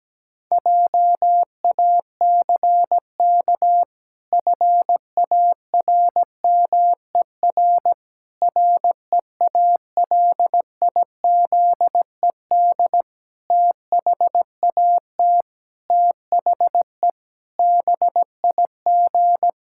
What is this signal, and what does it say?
Morse code, heard as a keyed tone, about 17 wpm, 700 Hz. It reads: JACK FARMER REALIZED THAT THE BIG
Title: Pangram Challenge v2 - 17wpm